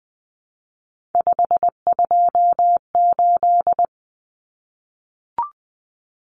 Morse code, heard as a keyed tone, about 20 wpm, 700 Hz. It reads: 528
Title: Common Cut Numbers - 3-Digit 20wpm